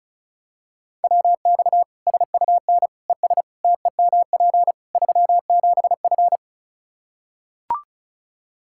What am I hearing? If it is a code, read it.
Morse code, heard as a keyed tone, about 34 wpm, 700 Hz. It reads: WX SUN ES TEMP 37F
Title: Practice QSO - Exchange 2 - Info - 35wpm